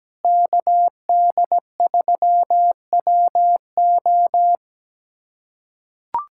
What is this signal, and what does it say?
Morse code, heard as a keyed tone, about 17 wpm, 700 Hz. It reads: KD3WO
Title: USA Callsign DE Callsign 17wpm